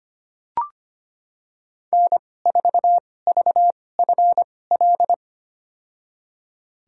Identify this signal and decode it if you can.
Morse code, heard as a keyed tone, about 25 wpm, spaced as for 18 wpm, 700 Hz. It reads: N4VFL
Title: USA Call Signs - 5 Characters - 2x Character Spacing - 25wpm